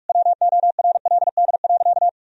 Morse code, heard as a keyed tone, about 46 wpm, 700 Hz. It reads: WORLD.
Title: Sentences from Top 1000 Words - Review Part 4 - 45wpm